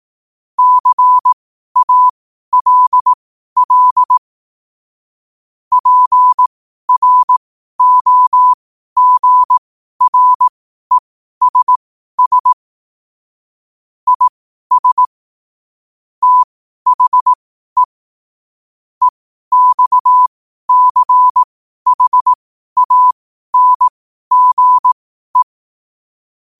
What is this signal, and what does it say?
Morse code, heard as a keyed tone, about 18 wpm, spaced as for 12 wpm, 1000 Hz. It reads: CALL PROGRESS IS THE EXCHANGE
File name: qotdfiles / 12wpm